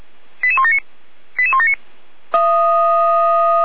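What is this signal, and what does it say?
Five-tone selective call (ZVEI), heard as quick runs of five tones, tones of about 80 ms. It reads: 88179 88179
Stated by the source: ZVEI five-tone selective call received on an SDR.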